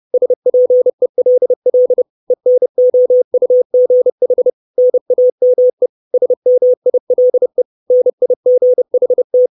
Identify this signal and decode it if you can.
Morse code, heard as a keyed tone, about 30 wpm, 500 Hz. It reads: SPELL ENOUGH NAME SMILE NIGHT
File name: 632 Words 30WPM